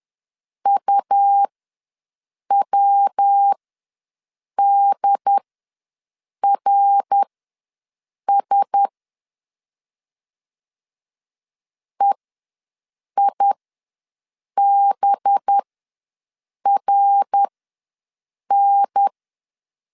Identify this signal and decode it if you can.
Morse code, heard as a keyed tone, about 10 wpm, spaced as for 6 wpm, 800 Hz. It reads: UWDRS EIBRN
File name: LEZIONI / 5 LBVC / 5 LB ws 10wpm 3616